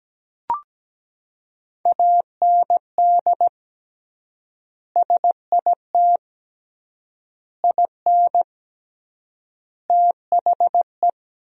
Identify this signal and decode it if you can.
Morse code, heard as a keyed tone, about 17 wpm, 700 Hz. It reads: AND SIT IN THE
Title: 3x Word-Spacing - Sentences from Top 500 Words - 17wpm